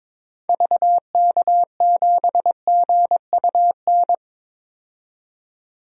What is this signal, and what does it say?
Morse code, heard as a keyed tone, about 22 wpm, 700 Hz. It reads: VK7GUN
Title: DXCC-100 Call Signs - 6 Characters - 22wpm